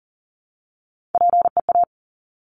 Morse code, heard as a keyed tone, about 38 wpm, 700 Hz. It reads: PEA
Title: Top Three Letter Combinations in Words 40wpm